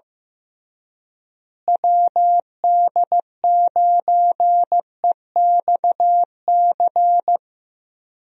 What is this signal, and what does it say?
Morse code, heard as a keyed tone, about 15 wpm, 700 Hz. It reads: WD9EXC
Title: USA Call Signs - 6 Characters - Repeated 3x - 15wpm